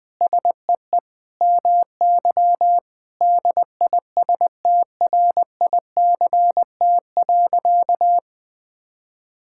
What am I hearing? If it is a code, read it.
Morse code, heard as a keyed tone, about 21 wpm, 700 Hz. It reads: SEE MY DISTRICT.
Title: Sentences from Top 1600 Words 20wpm